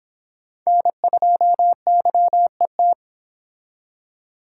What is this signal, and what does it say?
Morse code, heard as a keyed tone, about 25 wpm, 700 Hz. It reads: N2YET